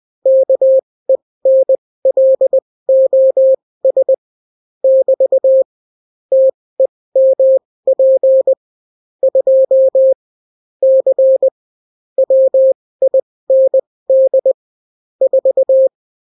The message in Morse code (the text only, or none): KENLOS <BT> TEMP 2 C WIND 4